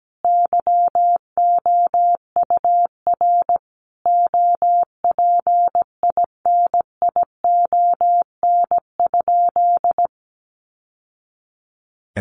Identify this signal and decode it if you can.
Morse code, heard as a keyed tone, about 17 wpm, 700 Hz. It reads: YOUR OPINION?
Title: Sentences from Top 1200 Words 17wpm